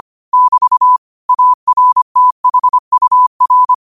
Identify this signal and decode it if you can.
Morse code, heard as a keyed tone, about 25 wpm, 1000 Hz. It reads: <BT> ARTHUR